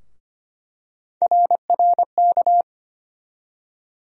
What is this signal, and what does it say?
Morse code, heard as a keyed tone, about 25 wpm, 700 Hz. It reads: RRK
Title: USA Call Sign Postfixes - 3 Letters 25wpm